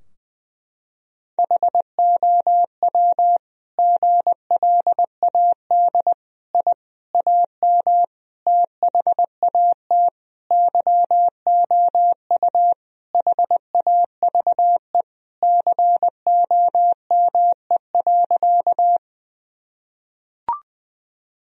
Morse code, heard as a keyed tone, about 19 wpm, 700 Hz. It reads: HOW GLAD I AM THAT YOU HAVE COME.
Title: Sentences from Top 1000 Words - Review Part 2 - 20wpm